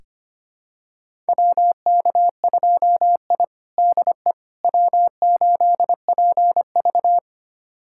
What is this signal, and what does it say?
Morse code, heard as a keyed tone, about 25 wpm, 700 Hz. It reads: WK2I DE W8PV
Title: USA Callsign DE Callsign - 4 Characters - 25wpm